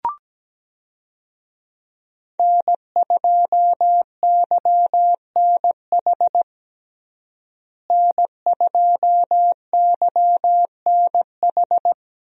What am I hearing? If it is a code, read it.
Morse code, heard as a keyed tone, about 17 wpm, 700 Hz. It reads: N2YNH N2YNH
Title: USA Call Signs - 5 Characters - Repeated 3x - 17wpm